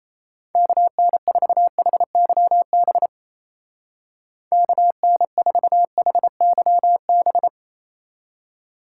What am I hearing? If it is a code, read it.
Morse code, heard as a keyed tone, about 34 wpm, 700 Hz. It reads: KN4HYB KN4HYB
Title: Speed Racing - USA 2x3 Call Signs 28wpm